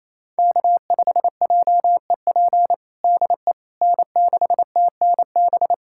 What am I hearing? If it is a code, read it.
Morse code, heard as a keyed tone, about 28 wpm, 700 Hz. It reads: K5JEP DE N6TNB
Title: USA Callsign DE Callsign - 5 Characters - 28wpm